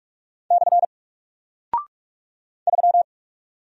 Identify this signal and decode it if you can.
Morse code, heard as a keyed tone, about 48 wpm, 700 Hz. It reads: / 3
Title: Single Character (Letter:Number:Punctuation:Prosign) Alt 45wpm